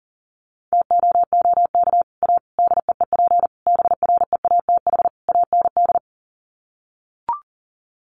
Morse code, heard as a keyed tone, about 40 wpm, 700 Hz. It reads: TOOK A DEEP BREATH AND
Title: Sets of 5 Words Encore 40wpm